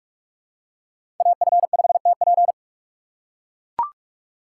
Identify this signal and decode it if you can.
Morse code, heard as a keyed tone, about 44 wpm, 690 Hz. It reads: AF5TP
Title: USA Call Signs - 5 Characters - 4x Character Spacing - 45wpm